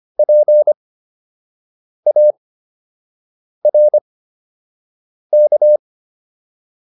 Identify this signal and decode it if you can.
Morse code, heard as a keyed tone, about 25 wpm, spaced as for 6 wpm, 600 Hz. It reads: PARK